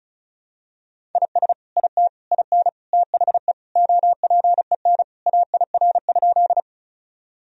Morse code, heard as a keyed tone, about 34 wpm, 700 Hz. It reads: IS IT IN THE OPEN AIR?
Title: Sentences from Top 1000 Words - Review Part 1 - 35wpm